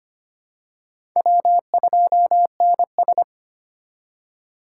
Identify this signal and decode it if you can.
Morse code, heard as a keyed tone, about 25 wpm, 700 Hz. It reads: W2NS
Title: USA Call Signs - 8x Character Spacing - 25wpm